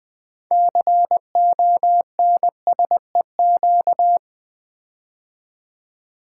Morse code, heard as a keyed tone, about 20 wpm, 700 Hz. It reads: CONSEQ
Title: Build A Word - 6 Letters At A Time - 12-Letter Words - 20wpm